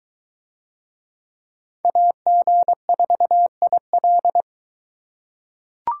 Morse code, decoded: AG4IL

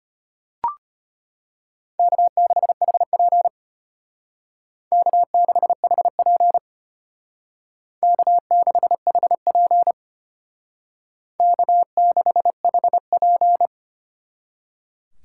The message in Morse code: K6HP K6HP K6HP K6HP

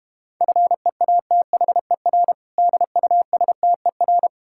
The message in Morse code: FEATHER DUSTER

